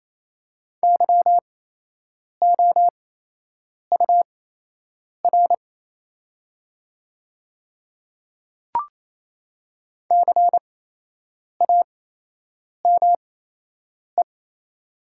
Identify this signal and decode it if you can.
Morse code, heard as a keyed tone, about 28 wpm, spaced as for 8 wpm, 700 Hz. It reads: YOUR CAME
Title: 8x Character Spacing - Top 300 Words - 28wpm